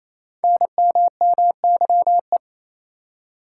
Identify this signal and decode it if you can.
Morse code, heard as a keyed tone, about 27 wpm, 700 Hz. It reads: NMMYE
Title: Code Groups - 5 Letters - 28wpm